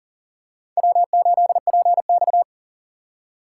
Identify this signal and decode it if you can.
Morse code, heard as a keyed tone, about 40 wpm, 700 Hz. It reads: W8PX
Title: USA Call Signs - 4 Characters - 2x Character Spacing - 40wpm